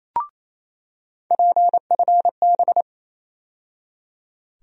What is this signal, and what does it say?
Morse code, heard as a keyed tone, about 28 wpm, 700 Hz. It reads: PFB
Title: USA Call Sign Postfixes - 3 Letters 28wpm